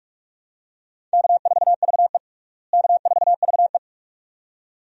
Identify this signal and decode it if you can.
Morse code, heard as a keyed tone, about 47 wpm, 700 Hz. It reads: K4VE K4VE
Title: USA Call Signs - 4 Characters - Repeated 2x - 45wpm